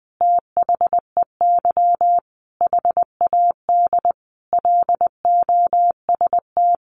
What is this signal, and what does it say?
Morse code, heard as a keyed tone, about 20 wpm, 700 Hz. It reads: THEY HAD LOST
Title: Sentences from Top 1400 Words 20wpm